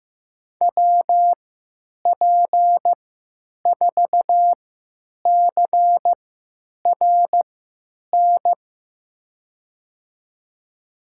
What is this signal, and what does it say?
Morse code, heard as a keyed tone, about 15 wpm, spaced as for 9 wpm, 700 Hz. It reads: WP4CRN
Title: USA Call Signs - 6 Characters - 3x Character Spacing - 15wpm